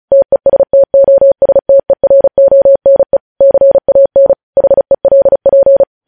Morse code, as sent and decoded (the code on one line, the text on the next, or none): - . ... - --- ... - . .-. --- -. / -.-. .- -. / .... .-.. .--.
TESTOSTERONE CAN HELP